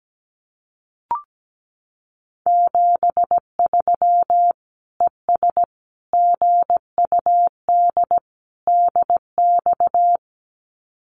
Morse code, heard as a keyed tone, about 17 wpm, 700 Hz. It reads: 73 ES GUD DX
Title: Common QSO Phrases 17wpm